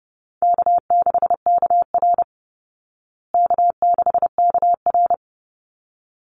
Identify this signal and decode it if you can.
Morse code, heard as a keyed tone, about 30 wpm, 700 Hz. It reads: K6KR K6KR
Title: Easy Call Signs Repeated 2x - 30wpm